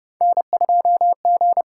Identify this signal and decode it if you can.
Morse code, heard as a keyed tone, about 32 wpm, 700 Hz. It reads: N2G
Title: USA Call Signs - 5 Characters - 8x Character Spacing - 30wpm